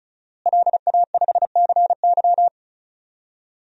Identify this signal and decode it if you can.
Morse code, heard as a keyed tone, about 36 wpm, 700 Hz. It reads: LA5CY
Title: DXCC-100 Call Signs - 5 Characters - Repeated 3x - 35wpm